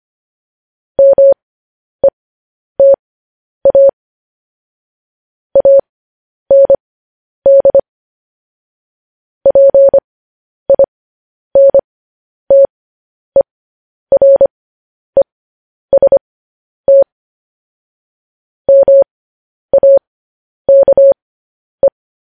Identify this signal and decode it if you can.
Morse code, heard as a keyed tone, about 25 wpm, spaced as for 10 wpm, 550 Hz. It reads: META AND PINTEREST MAKE
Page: news headlines in morse code at 10 wpm